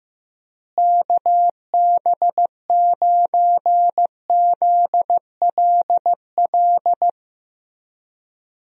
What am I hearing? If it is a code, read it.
Morse code, heard as a keyed tone, about 16 wpm, 700 Hz. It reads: KB9ZLL
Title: USA Call Signs - 6 Characters - Repeated 3x - 15wpm